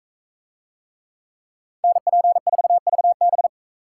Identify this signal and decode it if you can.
Morse code, heard as a keyed tone, about 41 wpm, 700 Hz. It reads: NP4VB